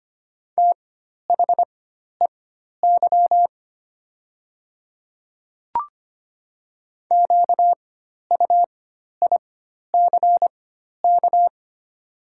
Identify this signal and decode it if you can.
Morse code, heard as a keyed tone, about 25 wpm, spaced as for 12 wpm, 690 Hz. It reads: THEY QUICK
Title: 4x Character Spacing - Top 500 Words - 25wpm